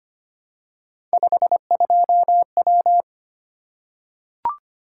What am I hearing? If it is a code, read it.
Morse code, heard as a keyed tone, about 25 wpm, 700 Hz. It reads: 52W
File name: Three Letter-Number Combinations - 25wpm